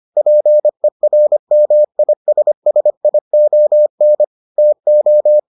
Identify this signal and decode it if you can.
Morse code, heard as a keyed tone, about 25 wpm, 600 Hz. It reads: PERMISSION TO